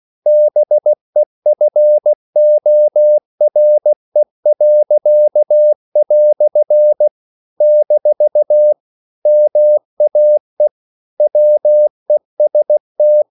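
Morse code, heard as a keyed tone, about 16 wpm, 600 Hz. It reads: BEFORE." - MAE WEST
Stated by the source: English quotations sent in CW as practice at 16 words per minute.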